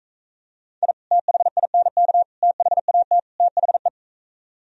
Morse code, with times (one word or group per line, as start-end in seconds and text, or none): 0.8-0.9 I
1.1-2.2 THINK
2.4-3.2 THAT
3.4-3.9 THE